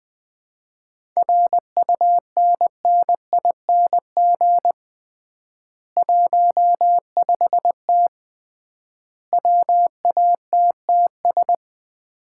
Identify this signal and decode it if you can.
Morse code, heard as a keyed tone, about 20 wpm, 700 Hz. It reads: RUNNING 15T WATTS
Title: Common QSO Phrases - 3x Word-Spacing - 20wpm